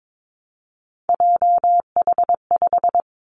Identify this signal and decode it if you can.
Morse code, heard as a keyed tone, about 22 wpm, 700 Hz. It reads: JH5
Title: Three Challenging Letter-Number Combinations 22wpm